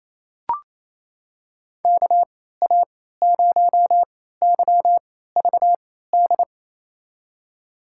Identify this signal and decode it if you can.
Morse code, heard as a keyed tone, about 28 wpm, spaced as for 16 wpm, 700 Hz. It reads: KA0YVD
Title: USA Call Signs - 3x Character Spacing - 28wpm